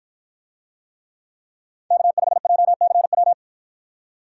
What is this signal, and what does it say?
Morse code, heard as a keyed tone, about 53 wpm, 700 Hz. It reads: K5JCW